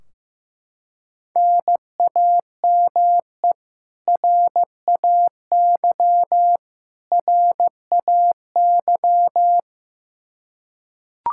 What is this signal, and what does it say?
Morse code, heard as a keyed tone, about 15 wpm, 700 Hz. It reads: NAME RAY RAY
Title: Practice QSO - Exchange 1 - Beginning - 15wpm